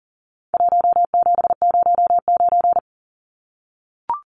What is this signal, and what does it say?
Morse code, heard as a keyed tone, about 40 wpm, 700 Hz. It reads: 1709
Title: Numbers - Contest Style - 4-Digit 40wpm